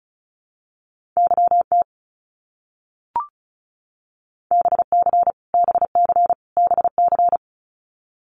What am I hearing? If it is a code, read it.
Morse code, heard as a keyed tone, about 36 wpm, 700 Hz. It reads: YT BC BC BC
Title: Canadian Provinces and Territories - Repeated 3x - 35wpm